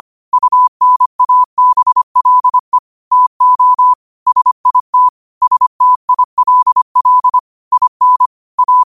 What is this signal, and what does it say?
Morse code, heard as a keyed tone, about 25 wpm, 1000 Hz. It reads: UNABLE TO SIT STILL IN A